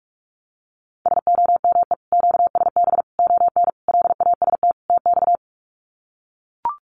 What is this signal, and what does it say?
Morse code, heard as a keyed tone, about 44 wpm, 700 Hz. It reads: SOME QSB ON LAST TX
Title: Common QSO Phrases 45wpm